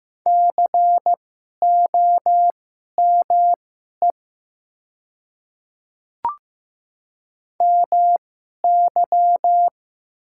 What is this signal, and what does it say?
Morse code, heard as a keyed tone, about 15 wpm, spaced as for 11 wpm, 700 Hz. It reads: COME MY
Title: 2x Character Spacing - Top 100 Words - 15wpm